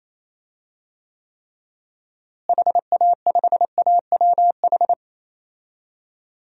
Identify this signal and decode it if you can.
Morse code, heard as a keyed tone, about 28 wpm, 700 Hz. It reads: HA5AWH
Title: DXCC-100 Call Signs - 6 Characters - Repeated 3x - 28wpm